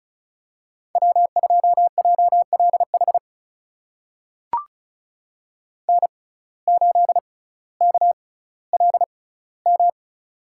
Morse code, heard as a keyed tone, about 36 wpm, spaced as for 18 wpm, 700 Hz. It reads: W2JLH N8KLM
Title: USA Call Signs - 5 Characters - 6x Character Spacing - 35wpm